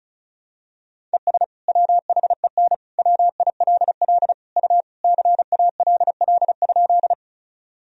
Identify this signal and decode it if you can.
Morse code, heard as a keyed tone, about 34 wpm, 700 Hz. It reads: ES WHEN WILL U CALL?
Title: Sentences from Top 100 Words and Abbreviations 35wpm